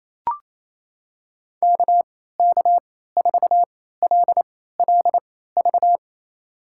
Morse code, heard as a keyed tone, about 27 wpm, spaced as for 16 wpm, 700 Hz. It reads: KK4LLV
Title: USA Call Signs - 3x Character Spacing - 28wpm